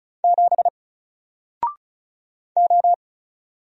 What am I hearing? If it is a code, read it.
Morse code, heard as a keyed tone, about 34 wpm, 700 Hz. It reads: 7 O
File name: Single-Letter-Number-35wpm